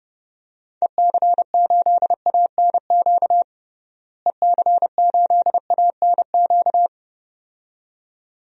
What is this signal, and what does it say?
Morse code, heard as a keyed tone, about 30 wpm, 700 Hz. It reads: EC8ANQ EC8ANQ
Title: DXCC-100 Call Signs - 6 Characters - Repeated 2x - 30wpm